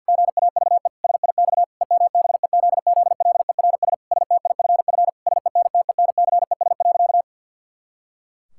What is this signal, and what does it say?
Morse code, heard as a keyed tone, about 50 wpm, 700 Hz. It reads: GAVE SIX EMBEZZLERS STIFF SENTENCES.
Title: Pangram Challenge v2 - 50wpm